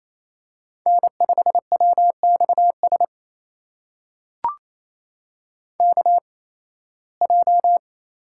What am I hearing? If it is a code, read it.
Morse code, heard as a keyed tone, about 28 wpm, spaced as for 18 wpm, 700 Hz. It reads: N5WXS KJ
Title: USA Call Signs - 5 Characters - 8x Character Spacing - 28wpm